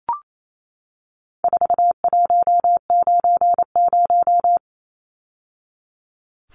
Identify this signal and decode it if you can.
Morse code, heard as a keyed tone, about 28 wpm, 700 Hz. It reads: 4190
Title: Numbers - 4-Digit 28wpm